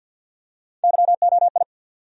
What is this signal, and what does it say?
Morse code, heard as a keyed tone, about 49 wpm, 700 Hz. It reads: YOI
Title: USA Call Sign Postfixes - 3 Letters 50wpm